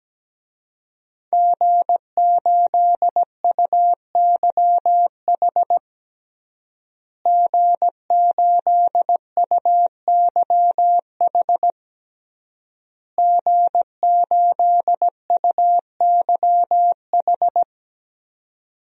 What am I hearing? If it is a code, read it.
Morse code, heard as a keyed tone, about 17 wpm, 700 Hz. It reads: G8UYH G8UYH G8UYH
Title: DXCC-100 Call Signs - 5 Characters - Repeated 3x - 17wpm